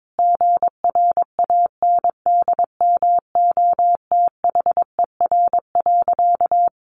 Morse code, heard as a keyed tone, about 22 wpm, 700 Hz. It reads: GRANDMOTHER.
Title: Sentences from Top 1200 Words 22wpm